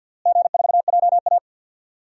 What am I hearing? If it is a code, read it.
Morse code, heard as a keyed tone, about 49 wpm, 700 Hz. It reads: G4JA
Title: DXCC-100 Call Signs - 4 Characters - Repeated 3x - 50wpm